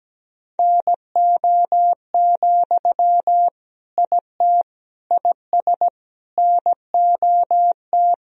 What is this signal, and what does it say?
Morse code, heard as a keyed tone, about 17 wpm, 700 Hz. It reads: NO, IT IS NOT
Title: Sentences from Top 1000 Words - Review Part 1 - 17wpm